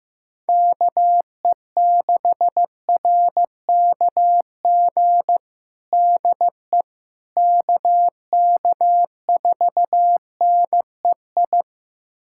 Morse code, heard as a keyed tone, about 15 wpm, 700 Hz. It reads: KE6RKG DE KK4NEI
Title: USA Callsign DE Callsign 15wpm